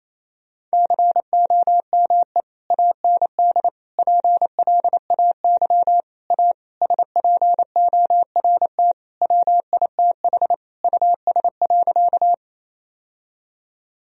Morse code, heard as a keyed tone, about 28 wpm, 700 Hz. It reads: COME AND PLAY A SPORT WITH US.
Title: Sentences from Top 1000 Words - Review Part 4 - 28wpm